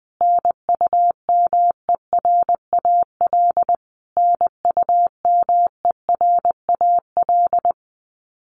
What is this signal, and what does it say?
Morse code, heard as a keyed tone, about 20 wpm, 700 Hz. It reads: NUMERAL NUMERAL
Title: Top 500 Words Repeated 20wpm